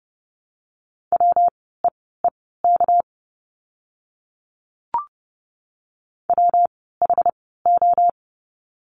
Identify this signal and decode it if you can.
Morse code, heard as a keyed tone, about 30 wpm, spaced as for 17 wpm, 700 Hz. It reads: WEEK WHO
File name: course / 30:10wpm / 230 Lesson 37f - Letter x - Words 30wpm